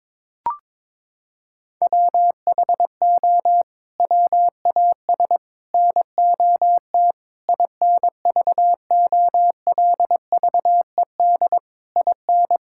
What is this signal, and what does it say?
Morse code, heard as a keyed tone, about 22 wpm, 700 Hz. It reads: WHO WAS NOT INVOLVED IN
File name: practice / 22wpm / Sets of 8 Words - 22wpm